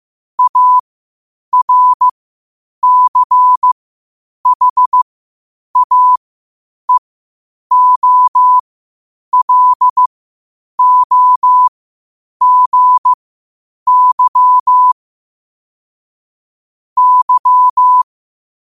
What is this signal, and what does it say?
Morse code, heard as a keyed tone, about 15 wpm, spaced as for 8 wpm, 1000 Hz. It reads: ARCHAEOLOGY Y
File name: qotdfiles / 8wpm